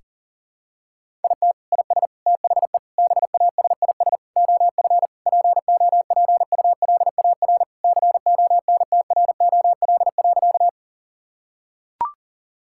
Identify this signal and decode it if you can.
Morse code, heard as a keyed tone, about 40 wpm, 700 Hz. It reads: IT IS THE BASIS OF POPULAR CONTROL.
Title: Sentences from Top 1000 Words - Review Part 5 - 40wpm